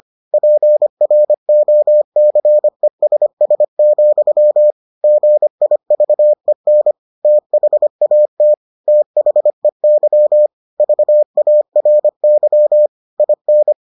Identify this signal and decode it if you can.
Morse code, heard as a keyed tone, about 25 wpm, 600 Hz. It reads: PROCESS, GIVEN THAT THEY VARY IN